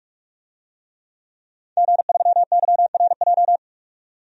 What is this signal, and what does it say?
Morse code, heard as a keyed tone, about 46 wpm, 700 Hz. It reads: G3YRJ
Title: DXCC-100 Call Signs - 5 Characters - 45wpm